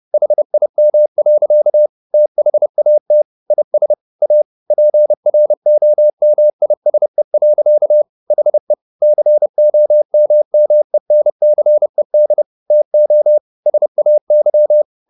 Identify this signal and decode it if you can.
Morse code, heard as a keyed tone, about 30 wpm, 600 Hz. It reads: HIM. THAT IS A PROMISE. HE COMMENCED TO SAY
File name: the old man and the sea / the old man and the sea 1 30wpm